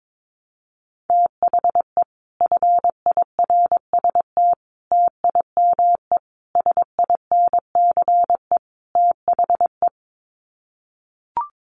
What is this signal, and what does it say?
Morse code, heard as a keyed tone, about 22 wpm, 700 Hz. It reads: THE FIRST TIME SINCE THE
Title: Sets of 5 Words Encore 22wpm